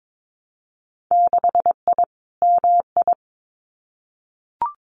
Morse code, heard as a keyed tone, about 22 wpm, 700 Hz. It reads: 6I MI
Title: Winter Field Day Exchanges Encore 22wpm